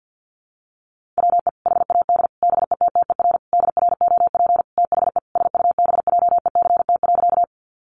Morse code, heard as a keyed tone, about 50 wpm, 700 Hz. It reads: WE HAD BETTER DROP THE SUBJECT.
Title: Sentences from Top 1400 Words 50wpm